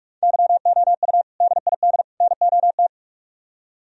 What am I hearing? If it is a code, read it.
Morse code, heard as a keyed tone, about 44 wpm, 700 Hz. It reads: YOU DID NOT